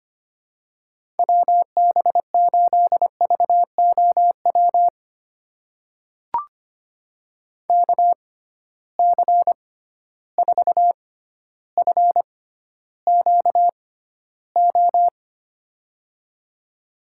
Morse code, heard as a keyed tone, about 25 wpm, spaced as for 13 wpm, 700 Hz. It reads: WB8VOW KC4FQO
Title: USA Call Signs - 6 Characters - 6x Character Spacing - 25wpm